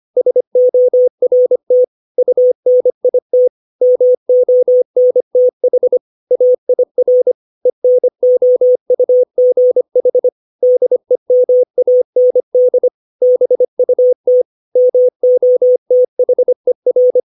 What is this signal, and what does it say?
Morse code, heard as a keyed tone, about 25 wpm, 500 Hz. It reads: SORT UNIT MONTH AIR ENOUGH DEMAND BUT MOTHER
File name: 632 words / 632 Words 25WPM